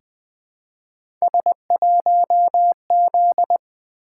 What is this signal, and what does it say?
Morse code, heard as a keyed tone, about 20 wpm, 700 Hz. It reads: S1Z
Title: Three Letter-Number Combinations - 20wpm